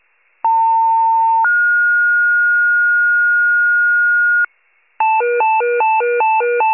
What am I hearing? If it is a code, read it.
Two-tone page, heard as a one-second tone, then a three-second tone, about 900 then 1430 Hz, with noise. It is heard from 0.4 to 4.5 s.
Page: 78 FIRE